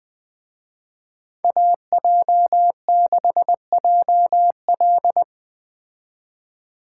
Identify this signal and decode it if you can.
Morse code, heard as a keyed tone, about 20 wpm, 700 Hz. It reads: AJ6JL
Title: Speed Racing - USA 2x2 Call Signs 17wpm